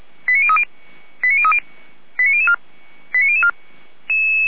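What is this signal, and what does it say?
Five-tone selective call (ZVEI), heard as quick runs of five tones, tones of about 80 ms. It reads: 89030 89030 89004 89004
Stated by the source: ZVEI five-tone call received on an SDR.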